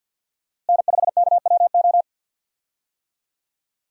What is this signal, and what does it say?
Morse code, heard as a keyed tone, about 51 wpm, 700 Hz. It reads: N5KWO
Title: USA Call Signs - 5 Characters - Repeated 3x - 50wpm